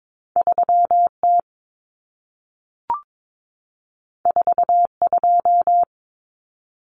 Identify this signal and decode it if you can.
Morse code, heard as a keyed tone, about 22 wpm, 700 Hz. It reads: 3T 42
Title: Common Cut Numbers - 2-Digit 22wpm